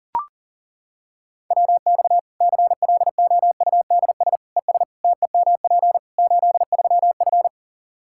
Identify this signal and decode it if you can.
Morse code, heard as a keyed tone, about 40 wpm, 700 Hz. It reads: WX CLOUDS ES TEMP 83F
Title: Practice QSO - Exchange 2 - Info - 40wpm